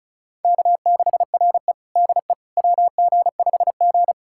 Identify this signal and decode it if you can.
Morse code, heard as a keyed tone, about 36 wpm, 700 Hz. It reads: K6RE DE WG5G
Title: USA Callsign DE Callsign - 4 Characters - 35wpm